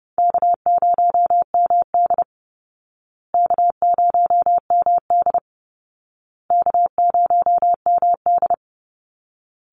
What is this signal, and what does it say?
Morse code, heard as a keyed tone, about 30 wpm, 700 Hz. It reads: K0MD K0MD K0MD
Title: Call Signs Repeated 2x - 30wpm